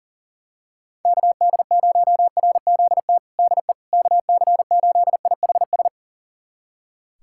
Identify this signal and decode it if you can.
Morse code, heard as a keyed tone, about 40 wpm, 700 Hz. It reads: KD0RZT DE KC8IHS
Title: USA Callsign DE Callsign 40wpm